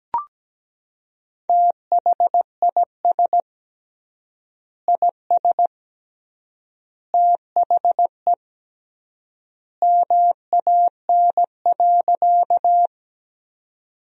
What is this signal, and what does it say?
Morse code, heard as a keyed tone, about 17 wpm, 700 Hz. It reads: THIS IS THE MAN.
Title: 3x Word-Spacing - Sentences from Top 200 Words - 17wpm